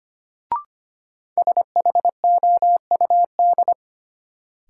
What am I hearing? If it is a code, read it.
Morse code, heard as a keyed tone, about 25 wpm, 700 Hz. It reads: SHOUD
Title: Sentences from Top 1000 Words - Review Part 4 - New Word Drill - 25wpm